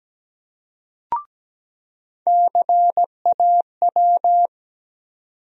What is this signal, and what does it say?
Morse code, heard as a keyed tone, about 17 wpm, 700 Hz. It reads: CAW
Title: USA Call Sign Postfixes - 3 Letters 17wpm